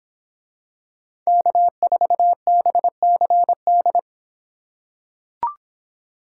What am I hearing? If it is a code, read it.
Morse code, heard as a keyed tone, about 24 wpm, 700 Hz. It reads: K4BCD